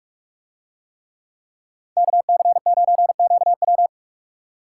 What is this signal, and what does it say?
Morse code, heard as a keyed tone, about 42 wpm, 700 Hz. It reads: KC9QW